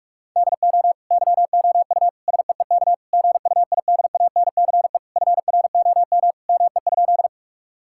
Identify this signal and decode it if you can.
Morse code, heard as a keyed tone, about 41 wpm, 700 Hz. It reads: DO YOU SEEK GUIDANCE FROM ME?
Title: Sentences from Top 1600 Words 45wpm